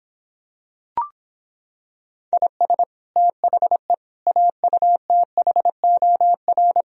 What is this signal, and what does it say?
Morse code, heard as a keyed tone, about 26 wpm, 700 Hz. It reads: IS THE AUTHOR